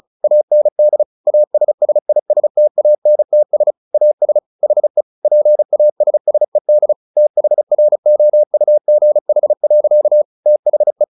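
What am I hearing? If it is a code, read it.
Morse code, heard as a keyed tone, about 34 wpm, 600 Hz. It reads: AND ASSISTANTS AS HE PASSED THROUGH. THE
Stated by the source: Morse practice at 35 wpm from CW med Gnister, sending Ken Follett's novel Never in English.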